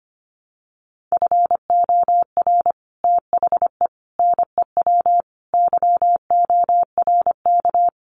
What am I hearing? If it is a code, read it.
Morse code, heard as a keyed tone, about 25 wpm, 700 Hz. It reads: FOR THE NEW YORK
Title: Sets of 5 Words 25wpm